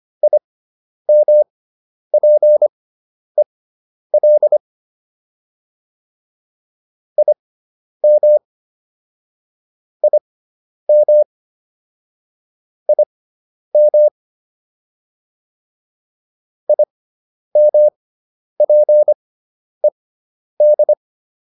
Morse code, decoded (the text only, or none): IMPEL IM IM IM IMPED